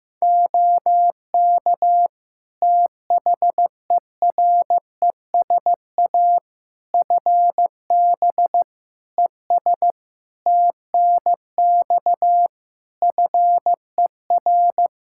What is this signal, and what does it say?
Morse code, heard as a keyed tone, about 16 wpm, 700 Hz. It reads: OK THERESA FB ES TNX FER
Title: Practice QSO - Exchange 2 - Info - 15wpm